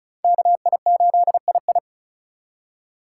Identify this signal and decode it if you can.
Morse code, heard as a keyed tone, about 36 wpm, 700 Hz. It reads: KI8II
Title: USA Call Signs - 5 Characters - 4x Character Spacing - 35wpm